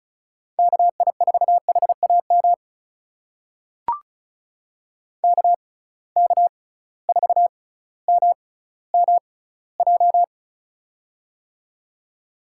Morse code, heard as a keyed tone, about 36 wpm, spaced as for 18 wpm, 700 Hz. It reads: KI4HAM KK4MMJ
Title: USA Call Signs - 6 Characters - 6x Character Spacing - 35wpm